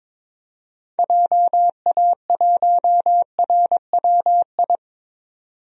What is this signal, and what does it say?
Morse code, heard as a keyed tone, about 22 wpm, 700 Hz. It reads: JA1RWI
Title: DXCC-100 Call Signs - 6 Characters - 22wpm